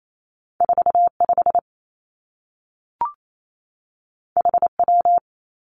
Morse code, heard as a keyed tone, about 28 wpm, 700 Hz. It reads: 4 5 H W
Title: Two Challenging Letter-Number Combinations 28wpm